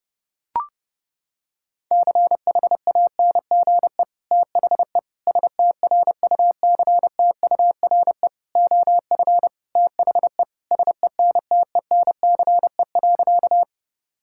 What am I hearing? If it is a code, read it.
Morse code, heard as a keyed tone, about 30 wpm, 700 Hz. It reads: CHANGE THE STRUCTURE OF THE SENTENCE.
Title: Sentences from Top 1000 Words - Review Part 6 - 30wpm